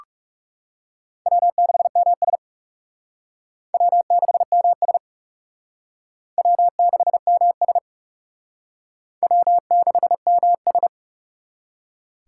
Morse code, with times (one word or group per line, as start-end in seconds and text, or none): 1.3-2.4 W6MS
3.7-5.0 W6MS
6.4-7.8 W6MS
9.2-10.9 W6MS